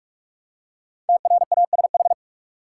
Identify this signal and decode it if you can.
Morse code, heard as a keyed tone, about 46 wpm, 700 Hz. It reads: TRASH
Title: Five Letter Words - 45wpm